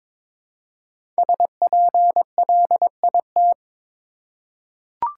Morse code, heard as a keyed tone, about 22 wpm, 690 Hz. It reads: SPLIT
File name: 5 Letter Words - First 3 Letters Given - 22wpm